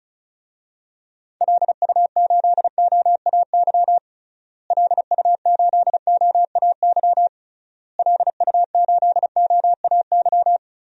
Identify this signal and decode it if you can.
Morse code, heard as a keyed tone, about 36 wpm, 700 Hz. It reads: LU8OAY LU8OAY LU8OAY
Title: DXCC-100 Call Signs - 6 Characters - Repeated 3x - 35wpm